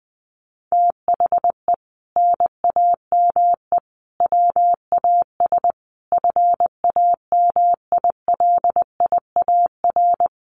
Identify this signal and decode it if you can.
Morse code, heard as a keyed tone, about 19 wpm, 700 Hz. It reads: THE NAME WAS FAMILIAR
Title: Sentences from Top 1000 Words 20wpm